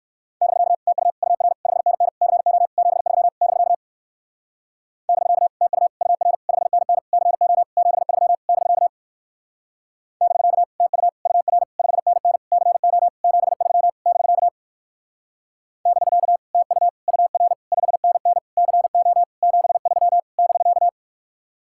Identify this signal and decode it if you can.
Morse code, heard as a keyed tone, about 56 wpm, 700 Hz. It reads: <BK> TU UR 5NN CO 73 <BK> <BK> TU UR 5NN CO 73 <BK> <BK> TU UR 5NN CO 73 <BK> <BK> TU UR 5NN CO 73 <BK>
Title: Speed-Racing Common QSO Phrases - 45wpm